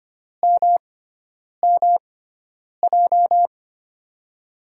Morse code, heard as a keyed tone, about 25 wpm, spaced as for 9 wpm, 700 Hz. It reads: MMJ
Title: USA Call Signs - 6 Characters - 6x Character Spacing - 25wpm